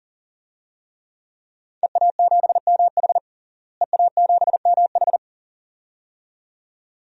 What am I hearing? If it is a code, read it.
Morse code, heard as a keyed tone, about 40 wpm, 700 Hz. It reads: EA7MH EA7MH
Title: DXCC-100 Call Signs - 5 Characters - Repeated 2x - 40wpm